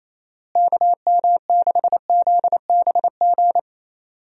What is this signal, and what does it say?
Morse code, heard as a keyed tone, about 27 wpm, 700 Hz. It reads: KM6ZBG
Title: USA Call Signs - 6 Characters - Repeated 2x - 28wpm